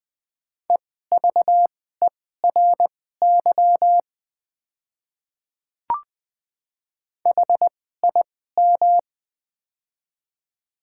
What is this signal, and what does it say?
Morse code, heard as a keyed tone, about 20 wpm, spaced as for 14 wpm, 700 Hz. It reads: EVERY HIM
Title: 2x Character Spacing - Top 200 Words - 20wpm